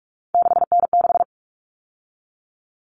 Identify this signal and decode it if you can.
Morse code, heard as a keyed tone, about 44 wpm, 700 Hz. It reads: 6N6
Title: All Cut Numbers - 3-Digit 45wpm